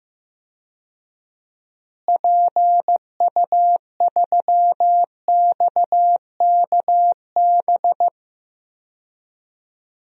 Morse code, heard as a keyed tone, about 15 wpm, 700 Hz. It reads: PU3XKB
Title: DXCC-100 Call Signs - 6 Characters - Repeated 2x - 15wpm